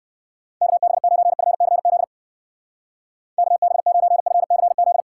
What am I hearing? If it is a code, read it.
Morse code, heard as a keyed tone, about 57 wpm, 700 Hz. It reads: KB9FCZ KB9FCZ